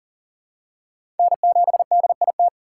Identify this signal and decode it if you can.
Morse code, heard as a keyed tone, about 39 wpm, 700 Hz. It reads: N7DIT
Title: USA Call Signs - 5 Characters - 3x Character Spacing - 40wpm